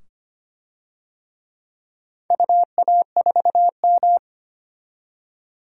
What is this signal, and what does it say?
Morse code, heard as a keyed tone, about 25 wpm, 700 Hz. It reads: UA4M
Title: DXCC-100 Call Signs - 4 Characters - 25wpm